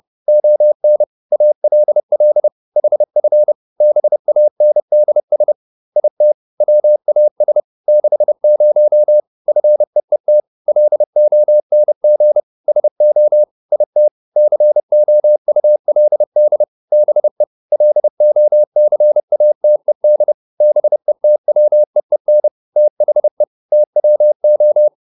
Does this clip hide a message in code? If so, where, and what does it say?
Yes: Morse code from 0.3 to 24.9 s, heard as a keyed tone, about 30 wpm, 600 Hz. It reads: ON ALL HF BANDS IT WAS 60 FEET LONG SO IT COULD BE LOCATED BETWEEN THE TWO